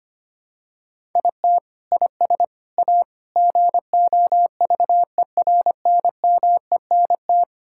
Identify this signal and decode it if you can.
Morse code, heard as a keyed tone, about 25 wpm, 700 Hz. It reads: IT IS A GOVERNMENT
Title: Sentences from Top 1000 Words - Review Part 5 - 25wpm